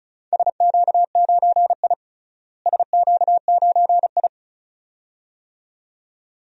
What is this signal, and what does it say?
Morse code, heard as a keyed tone, about 34 wpm, 700 Hz. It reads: SQ9I SQ9I
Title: DXCC-100 Call Signs - 4 Characters - Repeated 2x - 35wpm